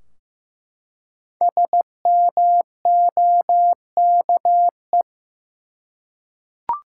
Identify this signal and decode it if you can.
Morse code, heard as a keyed tone, about 15 wpm, 700 Hz. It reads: SMOKE
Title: Five Letter Words - 15wpm